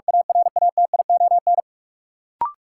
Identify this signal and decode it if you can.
Morse code, heard as a keyed tone, about 46 wpm, 700 Hz. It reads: ARATION
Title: Sentences from Top 1400 Words - New Word Drill - 45wpm